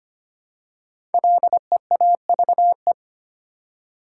Morse code, heard as a keyed tone, about 25 wpm, 700 Hz. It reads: LEAVE
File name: 5 Letter Words - First 3 Letters Given - 25wpm